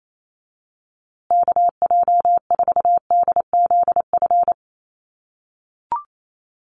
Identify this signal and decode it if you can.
Morse code, heard as a keyed tone, about 28 wpm, 700 Hz. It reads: KJ4DZF
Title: USA Call Signs Repeated 2x 28wpm